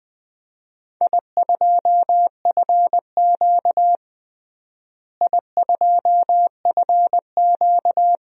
Morse code, heard as a keyed tone, about 20 wpm, 700 Hz. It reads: I2FQ I2FQ
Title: DXCC-100 Call Signs - 4 Characters - Repeated 3x - 20wpm